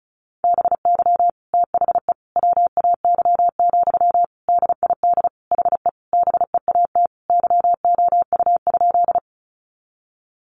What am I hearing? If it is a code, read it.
Morse code, heard as a keyed tone, about 34 wpm, 700 Hz. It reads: BY THE WAY, DID HE BEAT YOU?
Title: Sentences from Top 1300 Words 35wpm